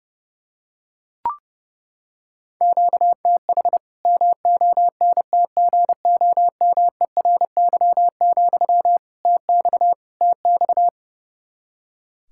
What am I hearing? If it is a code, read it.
Morse code, heard as a keyed tone, about 30 wpm, 700 Hz. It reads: QTH MONTGOMERY, TX TX
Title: K1USN SST QSO Practice 30wpm